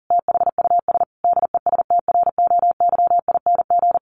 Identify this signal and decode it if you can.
Morse code, heard as a keyed tone, about 40 wpm, 700 Hz. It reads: THUS DESTROYING
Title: A Sound Of Thunder 40wpm